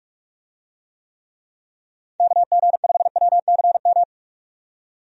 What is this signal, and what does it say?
Morse code, heard as a keyed tone, about 46 wpm, 700 Hz. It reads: KG5WCM
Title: USA Call Signs - 6 Characters - 45wpm